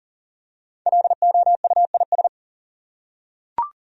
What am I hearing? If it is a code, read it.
Morse code, heard as a keyed tone, about 40 wpm, 700 Hz. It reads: LOUIS
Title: USA Common Names - 40wpm